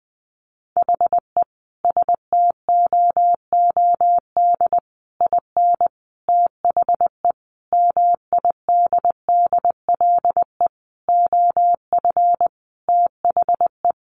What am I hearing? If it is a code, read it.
Morse code, heard as a keyed tone, about 20 wpm, 700 Hz. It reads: HE STOOD IN THE MIDDLE OF THE
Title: Sentences from Top 1400 Words 20wpm